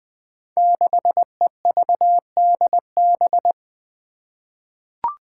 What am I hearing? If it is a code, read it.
Morse code, heard as a keyed tone, about 20 wpm, 700 Hz. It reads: 6EVDB
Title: Five Challenging Letter-Number Combinations 20wpm